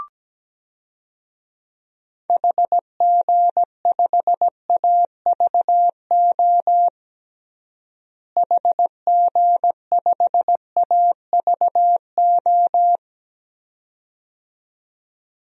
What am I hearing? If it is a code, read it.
Morse code, heard as a keyed tone, about 17 wpm, 700 Hz. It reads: HG5AVO HG5AVO
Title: DXCC-100 Call Signs - 6 Characters - Repeated 2x - 17wpm